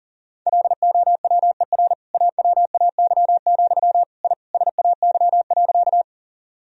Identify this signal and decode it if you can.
Morse code, heard as a keyed tone, about 40 wpm, 690 Hz. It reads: LOWER AWAY, I SAY.
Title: Sentences from Top 1000 Words - Review Part 4 - 40wpm